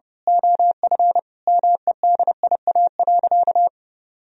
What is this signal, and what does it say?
Morse code, heard as a keyed tone, about 30 wpm, 700 Hz. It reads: OF MEDIA.
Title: Sentences from Top 1000 Words - Review Part 4 - 30wpm